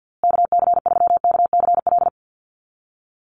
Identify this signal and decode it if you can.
Morse code, heard as a keyed tone, about 50 wpm, 700 Hz. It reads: KC3KCL
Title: USA Call Signs Repeated 3x 50wpm